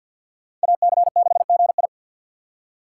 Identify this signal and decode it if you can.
Morse code, heard as a keyed tone, about 47 wpm, 700 Hz. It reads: AC6GI
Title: USA Call Signs - 5 Characters - 2x Character Spacing - 50wpm